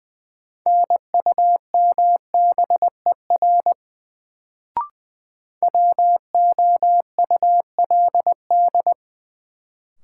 Morse code, heard as a keyed tone, about 21 wpm, 700 Hz. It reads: NUMBER WOULD
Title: Sets of 2 Words - New Word Drill - 20wpm